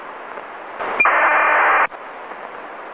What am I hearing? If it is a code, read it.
ACARS aircraft data.